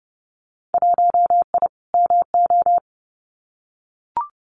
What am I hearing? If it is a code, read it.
Morse code, heard as a keyed tone, about 30 wpm, 700 Hz. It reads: 1I MO
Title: Winter Field Day Exchanges Encore 30wpm